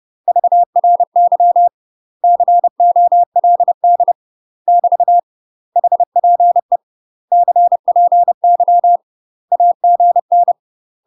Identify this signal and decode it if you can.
Morse code, heard as a keyed tone, about 30 wpm, 700 Hz. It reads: VRY COLD <BT> HPE CPY AGN
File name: UK MASTER QSOs TRAINING / 30WPM / UK MASTER QSOs 0036